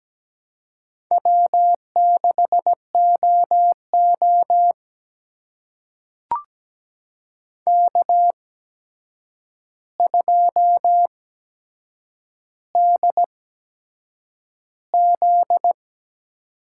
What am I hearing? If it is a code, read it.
Morse code, heard as a keyed tone, about 17 wpm, spaced as for 7 wpm, 700 Hz. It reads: W6OO K2DZ
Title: USA Call Signs - 4 Characters - 8x Character Spacing - 17wpm